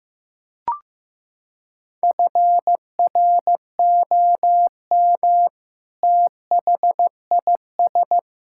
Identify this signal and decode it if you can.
Morse code, heard as a keyed tone, about 16 wpm, 700 Hz. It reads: FROM THIS
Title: Sentences from Top 1600 Words 15wpm